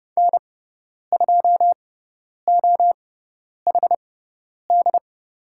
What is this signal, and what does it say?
Morse code, heard as a keyed tone, about 30 wpm, spaced as for 10 wpm, 700 Hz. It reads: N2OHD
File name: Lesson 16g - Letter c - Call Signs 30wpm